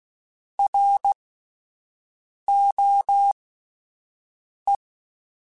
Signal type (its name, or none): Morse code